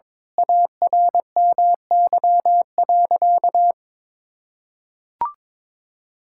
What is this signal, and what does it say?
Morse code, heard as a keyed tone, about 22 wpm, 700 Hz. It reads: ARMY.
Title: Sentences from Top 1000 Words - Review Part 6 - 22wpm